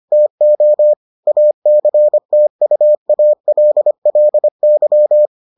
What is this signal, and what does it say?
Morse code, heard as a keyed tone, about 25 wpm, 600 Hz. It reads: TO ACTUALLY